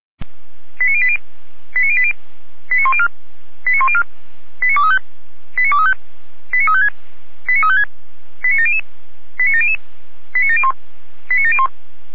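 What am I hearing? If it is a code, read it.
Five-tone selective call (ZVEI), heard as quick runs of five tones, tones of about 70 ms. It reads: 89080 89080 89104 89104 89235 89235 89356 89356 89700 89700 89701 89701
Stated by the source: Five-tone ZVEI selective call received on an SDR.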